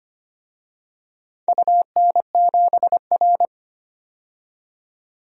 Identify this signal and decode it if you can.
Morse code, heard as a keyed tone, about 25 wpm, 690 Hz. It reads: UN7R
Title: DXCC-100 Call Signs - 4 Characters - 25wpm